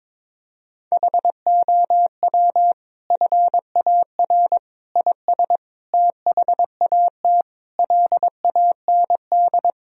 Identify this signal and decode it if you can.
Morse code, heard as a keyed tone, about 22 wpm, 700 Hz. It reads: HOW FAR IS THAT LAND
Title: Sentences from Top 1000 Words - Review Part 1 - 22wpm